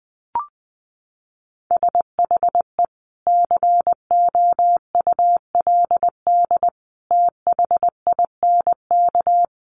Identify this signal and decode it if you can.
Morse code, heard as a keyed tone, about 20 wpm, 700 Hz. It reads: SHE COULD THINK
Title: Sentences from Top 800 Words 20wpm